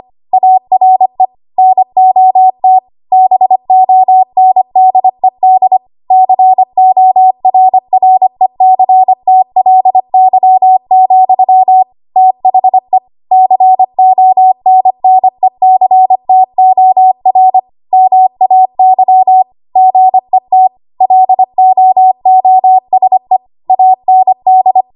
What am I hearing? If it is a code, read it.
Morse code, heard as a keyed tone, about 25 wpm, 750 Hz. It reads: ARE NOT BONDED CORRECTLY, THE CONNECTOR MAY GET LOOSE AND
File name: Archive / 25 WPM / 25WPM